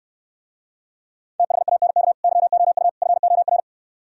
Morse code, heard as a keyed tone, about 69 wpm, 700 Hz. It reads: THANK YOU FOR